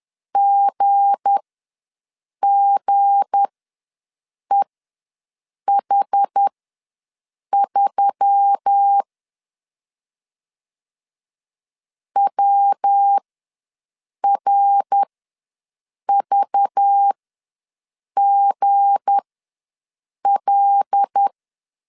Morse code, heard as a keyed tone, about 10 wpm, spaced as for 6 wpm, 800 Hz. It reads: GGEH3 WRVGL
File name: LEZIONI / 7 GF9Z / 7 G ws 10wpm 3843